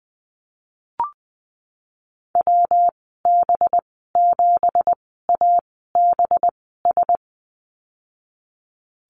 Morse code, heard as a keyed tone, about 19 wpm, spaced as for 14 wpm, 700 Hz. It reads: WB7ABS
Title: 193 Lesson 31g - Letter v - Call Signs 20wpm